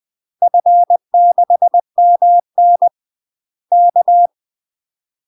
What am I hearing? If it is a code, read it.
Morse code, heard as a keyed tone, about 20 wpm, 700 Hz. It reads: F6MN K
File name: UK MASTER QSOs TRAINING / 20WPM / UK MASTER QSOs 0052